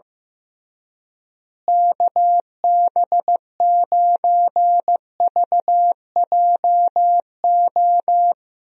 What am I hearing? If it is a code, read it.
Morse code, heard as a keyed tone, about 15 wpm, 700 Hz. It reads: KB9VJO
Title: USA Call Signs - 6 Characters - Repeated 3x - 15wpm